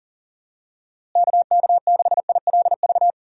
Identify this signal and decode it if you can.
Morse code, heard as a keyed tone, about 40 wpm, 700 Hz. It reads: KK6ILV
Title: USA Call Signs - 6 Characters - Repeated 2x - 40wpm